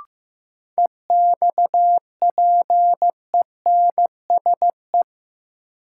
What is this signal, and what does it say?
Morse code, heard as a keyed tone, about 15 wpm, 700 Hz. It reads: EXPENSE
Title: Sets of 4 Words - New Word Drill - 15wpm